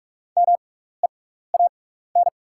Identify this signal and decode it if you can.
Morse code, heard as a keyed tone, about 44 wpm, spaced as for 15 wpm, 700 Hz. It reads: MEAN
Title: 6x Character Spacing - Top 500 Words - 45wpm